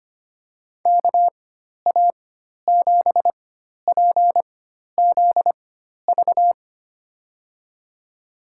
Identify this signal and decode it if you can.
Morse code, heard as a keyed tone, about 25 wpm, spaced as for 12 wpm, 700 Hz. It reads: KA7PZV
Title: USA Call Signs - 6 Characters - 4x Character Spacing - 25wpm